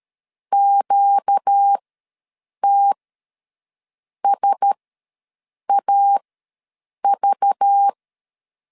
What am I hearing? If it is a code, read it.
Morse code, heard as a keyed tone, about 13 wpm, spaced as for 6 wpm, 800 Hz. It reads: QTSAV
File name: LEZIONI / 7 GF9Z / 7 GF9Z ws 12wpm 3321